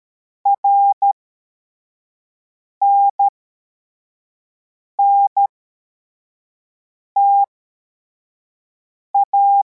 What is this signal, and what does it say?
Morse code, heard as a keyed tone, about 13 wpm, spaced as for 4 wpm, 800 Hz. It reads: RNNTA